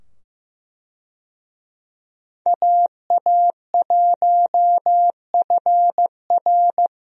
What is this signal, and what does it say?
Morse code, heard as a keyed tone, about 15 wpm, 700 Hz. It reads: AA1FR
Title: USA Call Signs - 5 Characters - 15wpm